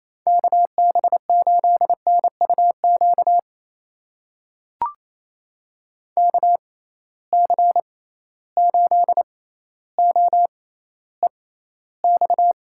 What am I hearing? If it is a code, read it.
Morse code, heard as a keyed tone, about 28 wpm, spaced as for 14 wpm, 700 Hz. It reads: KB8NUQ KC8OEX
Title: USA Call Signs - 6 Characters - 6x Character Spacing - 28wpm